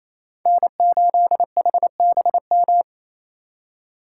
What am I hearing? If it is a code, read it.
Morse code, heard as a keyed tone, about 34 wpm, 700 Hz. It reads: N8HBM
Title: Speed Racing - USA 1x3 Call Signs 28wpm